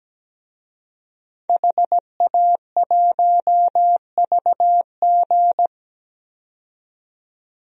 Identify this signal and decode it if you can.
Morse code, heard as a keyed tone, about 17 wpm, 700 Hz. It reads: HA1VG
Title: DXCC-100 Call Signs - 5 Characters - 17wpm